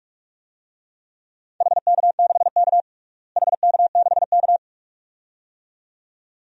Morse code, heard as a keyed tone, about 47 wpm, 700 Hz. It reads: HK6K HK6K
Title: DXCC-100 Call Signs - 4 Characters - Repeated 2x - 45wpm